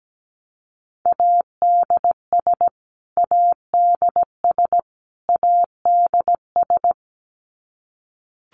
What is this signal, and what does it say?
Morse code, heard as a keyed tone, about 17 wpm, 700 Hz. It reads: ADS ADS ADS
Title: Abbreviations Repeated 3x - 17wpm